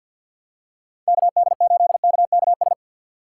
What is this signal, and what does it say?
Morse code, heard as a keyed tone, about 52 wpm, 700 Hz. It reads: KD8KKS
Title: USA Call Signs - 6 Characters - 4x Character Spacing - 50wpm